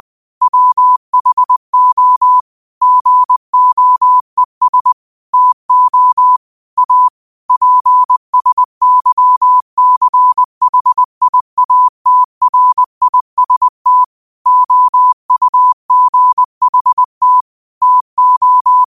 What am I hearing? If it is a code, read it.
Morse code, heard as a keyed tone, about 20 wpm, 1000 Hz. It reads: WHO GOES TO A PSYCHIATRIST OUGHT TO
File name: qotdfiles / 20wpm